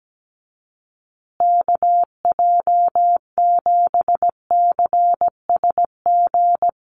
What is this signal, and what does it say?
Morse code, heard as a keyed tone, about 17 wpm, 700 Hz. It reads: KJ7CSG
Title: USA Call Signs Repeated 2x 17wpm